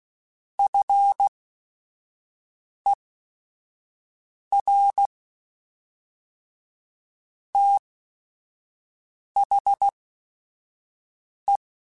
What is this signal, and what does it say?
Morse code, heard as a keyed tone, about 16 wpm, spaced as for 5 wpm, 800 Hz. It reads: FER THE